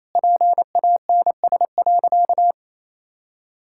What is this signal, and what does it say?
Morse code, heard as a keyed tone, about 28 wpm, 700 Hz. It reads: PANS.
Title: Pangram Challenge v2 - 28wpm